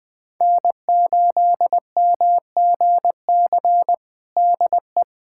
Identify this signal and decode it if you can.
Morse code, heard as a keyed tone, about 20 wpm, 700 Hz. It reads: N8MGC DE
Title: USA Callsign DE Callsign 20wpm